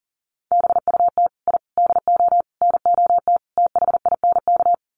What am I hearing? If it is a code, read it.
Morse code, heard as a keyed tone, about 40 wpm, 700 Hz. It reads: BUT I DO NOT THINK
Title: Sets of 5 Words 40wpm